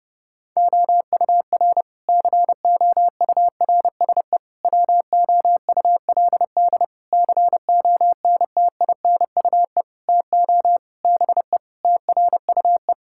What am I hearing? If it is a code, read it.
Morse code, heard as a keyed tone, about 30 wpm, 700 Hz. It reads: OUR COURSE WOULD CONTINUE TO BE TRUE
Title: Sentences from Top 1000 Words - Review Part 6 - 30wpm